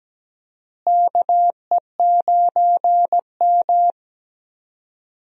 Morse code, heard as a keyed tone, about 17 wpm, 700 Hz. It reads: KE9M
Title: USA Call Signs - 4 Characters - Repeated 3x - 17wpm